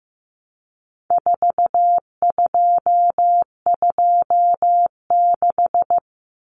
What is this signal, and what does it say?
Morse code, heard as a keyed tone, about 15 wpm, 700 Hz. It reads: 4226
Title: Common Cut Numbers - 4-Digit 15wpm